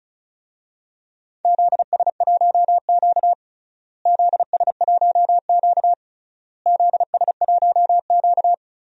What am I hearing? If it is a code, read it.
Morse code, heard as a keyed tone, about 34 wpm, 700 Hz. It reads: ZS1Q ZS1Q ZS1Q
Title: DXCC-100 Call Signs - 4 Characters - Repeated 3x - 35wpm